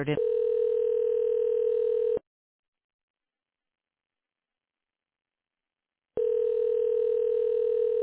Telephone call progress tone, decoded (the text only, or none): ringback tone